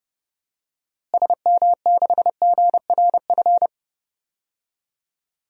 Morse code, heard as a keyed tone, about 30 wpm, 700 Hz. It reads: SM6GRF
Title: DXCC-100 Call Signs - 6 Characters - Repeated 2x - 30wpm